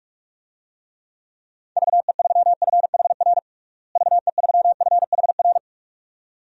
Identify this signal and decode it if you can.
Morse code, heard as a keyed tone, about 41 wpm, 700 Hz. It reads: VE3FHR VE3FHR
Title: DXCC-100 Call Signs - 6 Characters - Repeated 2x - 45wpm